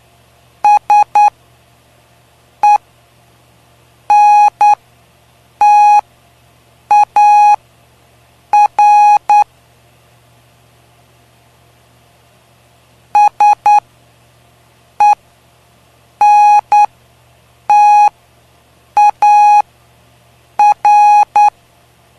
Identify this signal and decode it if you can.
Morse code, heard as a keyed tone, about 9 wpm, spaced as for 6 wpm, 830 Hz. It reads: SENTAR SENTAR
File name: cw / py2lcd / 05-PY2LCD-Li��o 5